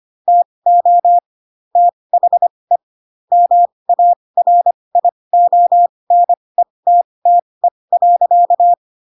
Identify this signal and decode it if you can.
Morse code, heard as a keyed tone, about 25 wpm, spaced as for 20 wpm, 700 Hz. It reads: TO THE MARIONETTE.
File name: Pinocchio / Pinocchio ch7 15 25 ik1hge